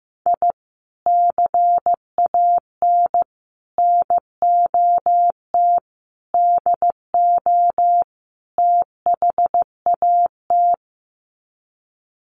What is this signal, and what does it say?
Morse code, heard as a keyed tone, about 15 wpm, 700 Hz. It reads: I CAN NOT DO THAT
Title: Sets of 5 Words Encore2 15wpm